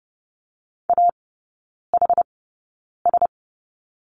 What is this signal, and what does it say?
Morse code, heard as a keyed tone, about 30 wpm, spaced as for 9 wpm, 700 Hz. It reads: AHS
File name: Lesson 17e - Letter u - Three Characters 30wpm